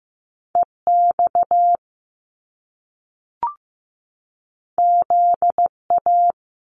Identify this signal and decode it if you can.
Morse code, heard as a keyed tone, about 15 wpm, 700 Hz. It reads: EX ZA
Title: Top Two Letter Combinations in Words 15wpm